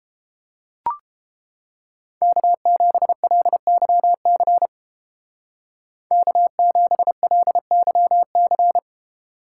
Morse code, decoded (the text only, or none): K7LYC K7LYC